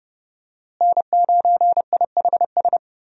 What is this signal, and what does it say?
Morse code, heard as a keyed tone, about 32 wpm, 700 Hz. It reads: N9IHS
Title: USA Call Signs - 5 Characters - Repeated 3x - 30wpm